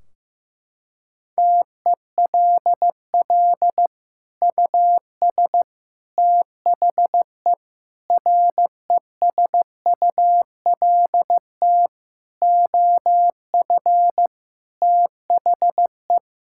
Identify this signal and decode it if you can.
Morse code, heard as a keyed tone, about 15 wpm, 700 Hz. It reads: TELL US THE RESULT OF THE
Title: Sentences from Top 1600 Words 15wpm